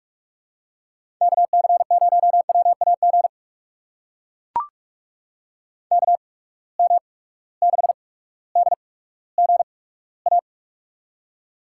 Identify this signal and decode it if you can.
Morse code, heard as a keyed tone, about 43 wpm, spaced as for 19 wpm, 700 Hz. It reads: KC0WAG KM6DGA
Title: USA Call Signs - 6 Characters - 8x Character Spacing - 45wpm